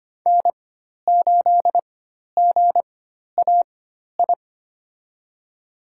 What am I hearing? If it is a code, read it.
Morse code, heard as a keyed tone, about 25 wpm, spaced as for 12 wpm, 700 Hz. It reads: N8GAI
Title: USA Call Signs - 5 Characters - 4x Character Spacing - 25wpm